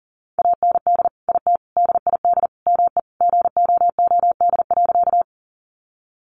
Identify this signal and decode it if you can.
Morse code, heard as a keyed tone, about 37 wpm, 700 Hz. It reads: AND IT DID ME GOOD.
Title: Sentences from Top 200 Words 40wpm